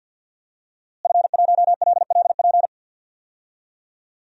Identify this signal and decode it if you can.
Morse code, heard as a keyed tone, about 52 wpm, 700 Hz. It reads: F1LLP
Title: DXCC-100 Call Signs - 5 Characters - Repeated 2x - 50wpm